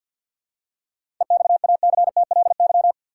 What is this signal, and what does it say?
Morse code, heard as a keyed tone, about 50 wpm, 700 Hz. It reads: EXACTLY
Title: Seven Letter Words - 50wpm